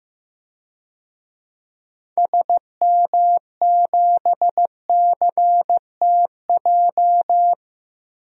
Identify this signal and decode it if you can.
Morse code, heard as a keyed tone, about 15 wpm, 700 Hz. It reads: SM7CTJ
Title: DXCC-100 Call Signs - 6 Characters - Repeated 2x - 15wpm